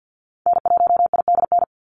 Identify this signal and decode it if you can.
Morse code, heard as a keyed tone, about 48 wpm, 700 Hz. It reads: N1IDN
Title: NA SOTA Activator Callsigns 50wpm